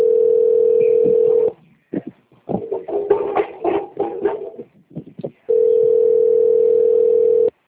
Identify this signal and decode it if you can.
Ringback tone (North America), 440 + 480 Hz.